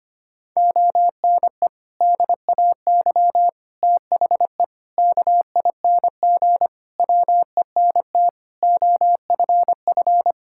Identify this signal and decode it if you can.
Morse code, heard as a keyed tone, about 25 wpm, 700 Hz. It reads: ONE DAY THE KING WENT OFF
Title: Sentences from Top 1600 Words 25wpm